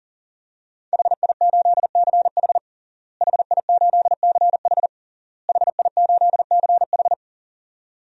Morse code, heard as a keyed tone, about 40 wpm, 700 Hz. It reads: HI8CH HI8CH HI8CH
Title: DXCC-100 Call Signs - 5 Characters - Repeated 3x - 40wpm